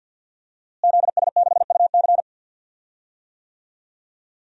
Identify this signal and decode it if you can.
Morse code, heard as a keyed tone, about 47 wpm, 700 Hz. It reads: ZS6UC
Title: DXCC-100 Call Signs - 5 Characters - Repeated 2x - 50wpm